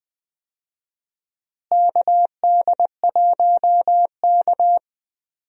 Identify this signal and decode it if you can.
Morse code, heard as a keyed tone, about 20 wpm, 690 Hz. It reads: KD1K
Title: USA Call Signs - 4 Characters - Repeated 3x - 20wpm